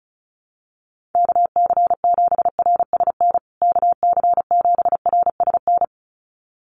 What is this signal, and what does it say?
Morse code, heard as a keyed tone, about 34 wpm, 700 Hz. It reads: KC7RSN KC7RSN
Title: USA Call Signs Repeated 2x 35wpm